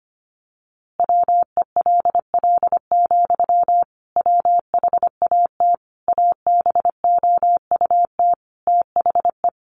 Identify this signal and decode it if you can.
Morse code, heard as a keyed tone, about 25 wpm, 700 Hz. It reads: WELL, WHAT ABOUT THE
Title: A Sound Of Thunder 25wpm